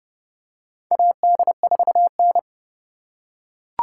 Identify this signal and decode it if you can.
Morse code, heard as a keyed tone, about 29 wpm, 700 Hz. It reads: AD4N
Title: USA Call Signs - 4 Characters - 4x Character Spacing - 30wpm